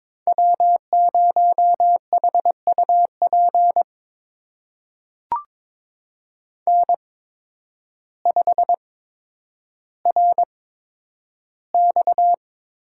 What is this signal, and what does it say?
Morse code, heard as a keyed tone, about 22 wpm, spaced as for 10 wpm, 700 Hz. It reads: W0HUP N5RX
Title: USA Call Signs - 5 Characters - 8x Character Spacing - 22wpm